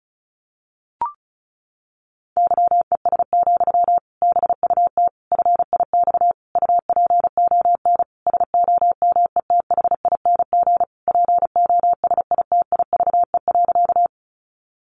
Morse code, heard as a keyed tone, about 34 wpm, 700 Hz. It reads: YES, BUT FIX UPON SOMETHING POSITIVE.
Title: Sentences from Top 1400 Words 35wpm